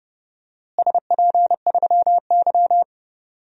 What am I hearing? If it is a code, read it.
Morse code, heard as a keyed tone, about 30 wpm, 700 Hz. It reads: SP3Y
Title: DXCC-100 Call Signs - 4 Characters - Repeated 3x - 30wpm